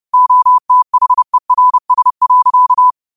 Morse code, heard as a keyed tone, about 30 wpm, 1000 Hz. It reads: OTHERS.
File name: qotdfiles / 30wpm